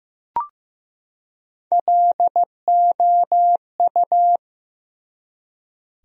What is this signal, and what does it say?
Morse code, heard as a keyed tone, about 15 wpm, 700 Hz. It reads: LOU
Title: USA Common Names - Max 4 Letters - 15wpm